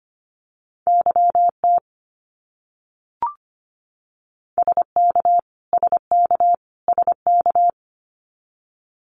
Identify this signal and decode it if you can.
Morse code, heard as a keyed tone, about 25 wpm, 700 Hz. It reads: YT SK SK SK
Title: Canadian Provinces and Territories - Repeated 3x - 25wpm